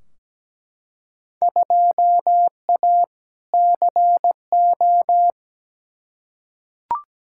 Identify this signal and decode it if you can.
Morse code, heard as a keyed tone, about 17 wpm, 700 Hz. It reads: 2A CO
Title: Field Day Exchanges - 17wpm